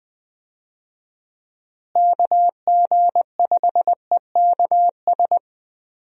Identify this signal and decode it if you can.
Morse code, heard as a keyed tone, about 20 wpm, 700 Hz. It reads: KG5EKS
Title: USA Call Signs - 6 Characters - Repeated 2x - 20wpm